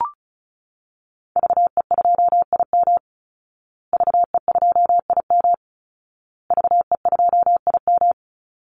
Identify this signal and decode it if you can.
Morse code, heard as a keyed tone, about 34 wpm, 700 Hz. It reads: VE2IM VE2IM VE2IM
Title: Call Signs Repeated 2x - 35wpm